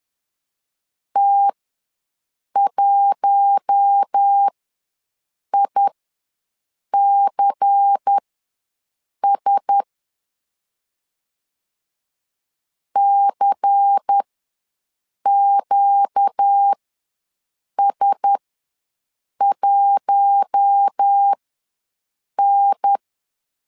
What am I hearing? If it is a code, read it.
Morse code, heard as a keyed tone, about 10 wpm, spaced as for 6 wpm, 800 Hz. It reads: T1ICS CQS1N
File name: LEZIONI / 5 LBVC / 5 LBVC ws 10wpm 3733